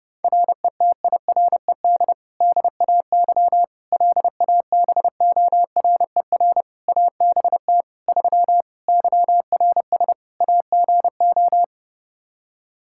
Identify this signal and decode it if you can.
Morse code, heard as a keyed tone, about 30 wpm, 700 Hz. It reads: RETIRED DAY LABORER ABT 3 YRS AGO
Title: Practice QSO - Exchange 3 - More Info - 30wpm